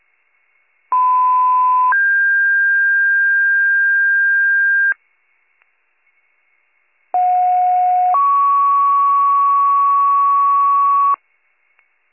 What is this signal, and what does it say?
Two-tone page, heard as a one-second tone, then a three-second tone, about 1010 then 1600 Hz, about 730 then 1120 Hz. It is heard from 0.9 to 11.1 s.